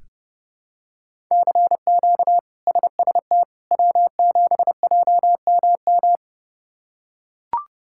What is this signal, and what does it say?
Morse code, heard as a keyed tone, about 30 wpm, 700 Hz. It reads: CQ SST W7JMM
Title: K1USN SST QSO Practice - 30wpm